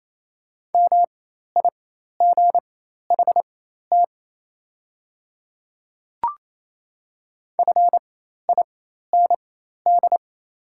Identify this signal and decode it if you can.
Morse code, heard as a keyed tone, about 27 wpm, spaced as for 13 wpm, 690 Hz. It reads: MIGHT FIND
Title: 4x Character Spacing - Top 300 Words - 28wpm